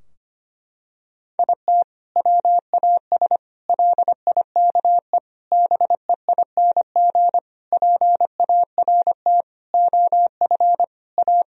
Morse code, heard as a keyed tone, about 25 wpm, 700 Hz. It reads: IT WAS LIKE BEING PART OF A
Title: Sentences from Top 1000 Words - Review Part 3 - 25wpm